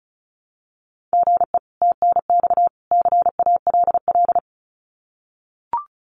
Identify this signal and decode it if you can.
Morse code, heard as a keyed tone, about 34 wpm, 700 Hz. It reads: GE TNX CALL
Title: Common QSO Phrases 35wpm